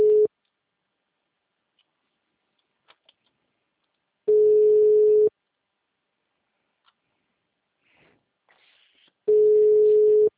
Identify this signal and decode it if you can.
Ringback tone (Europe, 425 Hz).